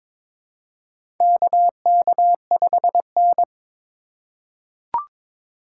Morse code, heard as a keyed tone, about 22 wpm, 700 Hz. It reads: KK5N
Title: USA Call Signs - 2x Character Spacing - 22wpm